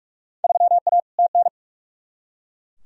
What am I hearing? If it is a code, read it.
Morse code, heard as a keyed tone, about 43 wpm, 700 Hz. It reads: 3A TN
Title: Field Day Exchanges - 45wpm